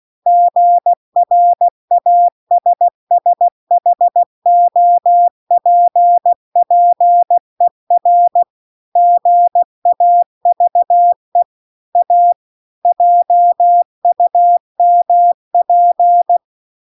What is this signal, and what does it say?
Morse code, heard as a keyed tone, about 16 wpm, 700 Hz. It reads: GRASSHOPPER GAVE A JUMP
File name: CW-The-Big-Twoheaded-River-2-16WPM